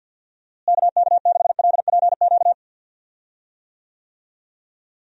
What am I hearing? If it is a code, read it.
Morse code, heard as a keyed tone, about 51 wpm, 700 Hz. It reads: KK6LPQ
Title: USA Call Signs - 6 Characters - Repeated 2x - 50wpm